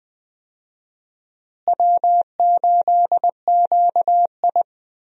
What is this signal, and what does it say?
Morse code, heard as a keyed tone, about 21 wpm, 700 Hz. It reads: W8QI
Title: USA Call Signs - 4 Characters - Repeated 3x - 20wpm